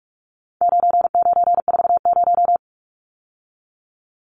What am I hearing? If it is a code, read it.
Morse code, heard as a keyed tone, about 44 wpm, 700 Hz. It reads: 9940
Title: Numbers - 4-Digit 45wpm